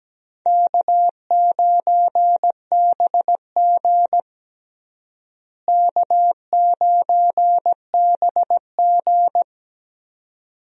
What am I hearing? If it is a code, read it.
Morse code, heard as a keyed tone, about 17 wpm, 700 Hz. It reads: K9BG K9BG
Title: USA Call Signs - 4 Characters - Repeated 2x - 17wpm